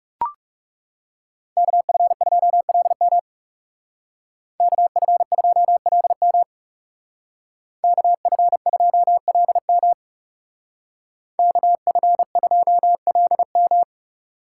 Morse code, heard as a keyed tone, about 37 wpm, 700 Hz. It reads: KF2LM KF2LM KF2LM KF2LM
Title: Speed Racing - USA 2x2 Call Signs 30wpm